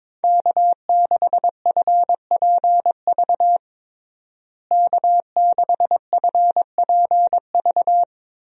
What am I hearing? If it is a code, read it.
Morse code, heard as a keyed tone, about 22 wpm, 700 Hz. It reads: K6FPV K6FPV
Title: USA Call Signs - 5 Characters - Repeated 3x - 22wpm